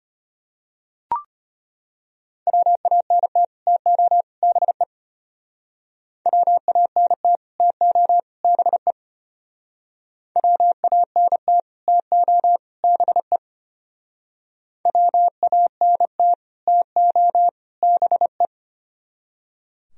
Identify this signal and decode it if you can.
Morse code, heard as a keyed tone, about 32 wpm, 700 Hz. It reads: WANT TO BE WANT TO BE WANT TO BE WANT TO BE